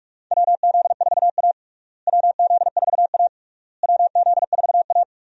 Morse code, heard as a keyed tone, about 41 wpm, 700 Hz. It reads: WZ4A WZ4A WZ4A
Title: USA Call Signs - 4 Characters - Repeated 3x - 45wpm